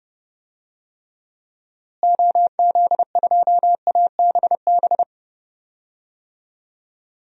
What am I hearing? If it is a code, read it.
Morse code, heard as a keyed tone, about 30 wpm, 700 Hz. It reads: OZ2ABB
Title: DXCC-100 Call Signs - 6 Characters - Repeated 3x - 30wpm